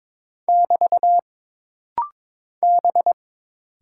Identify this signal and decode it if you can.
Morse code, heard as a keyed tone, about 22 wpm, 700 Hz. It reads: <BT> B
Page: This rapid-fire timing is geared to push your ICR (Instant Character Recognition) skills.